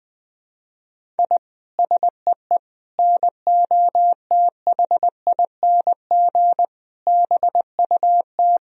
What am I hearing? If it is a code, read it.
Morse code, heard as a keyed tone, about 20 wpm, 700 Hz. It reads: I SEE NOTHING BUT
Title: Sentences from Top 1000 Words - Review Part 1 - 20wpm